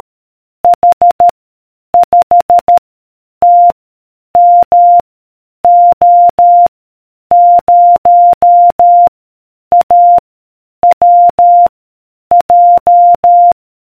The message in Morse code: H 5 T M O 0 A W J